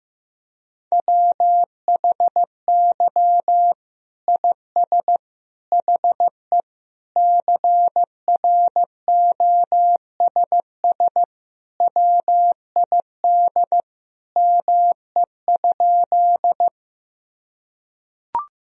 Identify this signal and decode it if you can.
Morse code, heard as a keyed tone, about 15 wpm, 700 Hz. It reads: WHY IS HE CROSS WID ME?
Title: Sentences from Top 300 Words and Abbreviations 15wpm